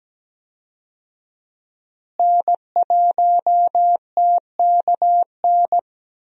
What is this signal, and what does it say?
Morse code, heard as a keyed tone, about 17 wpm, 700 Hz. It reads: N1TKN
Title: USA Call Signs - 5 Characters - Repeated 2x - 17wpm